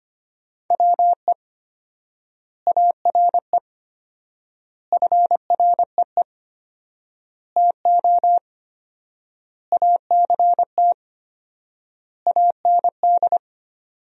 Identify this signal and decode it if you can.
Morse code, heard as a keyed tone, about 25 wpm, 700 Hz. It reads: WE ARE FREE TO ACT AND
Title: 4x Word-Spacing - Sentences from Top 500 Words - 25wpm